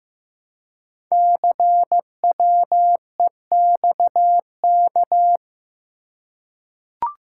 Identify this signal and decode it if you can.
Morse code, heard as a keyed tone, about 15 wpm, 700 Hz. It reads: CWEXK
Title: Code Groups - 5 Letters - 15wpm